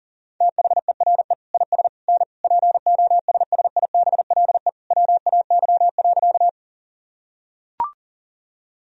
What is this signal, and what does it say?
Morse code, heard as a keyed tone, about 40 wpm, 700 Hz. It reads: THERE IS N POSSIBLE WAY.
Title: Sentences from Top 500 Words and Abbreviations 40wpm